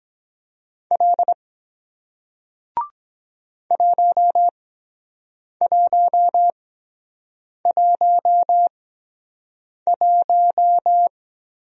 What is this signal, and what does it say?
Morse code, heard as a keyed tone, about 23 wpm, spaced as for 7 wpm, 700 Hz. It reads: L 1111